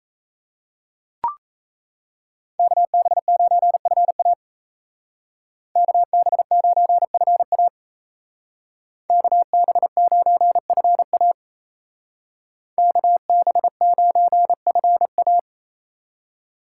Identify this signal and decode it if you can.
Morse code, heard as a keyed tone, about 32 wpm, 700 Hz. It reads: KB9FA KB9FA KB9FA KB9FA
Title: Speed Racing - USA 2x2 Call Signs 28wpm